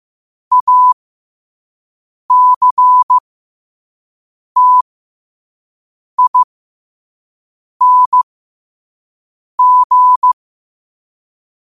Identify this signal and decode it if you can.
Morse code, heard as a keyed tone, about 15 wpm, spaced as for 5 wpm, 1000 Hz. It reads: ACTING
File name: qotdfiles / 5wpm